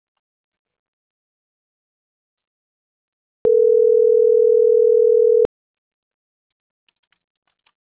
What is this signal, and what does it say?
Ringback tone (North America), 440 + 480 Hz.
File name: CENTRAL NAVIERAS / out onelinecapacitacion - .78687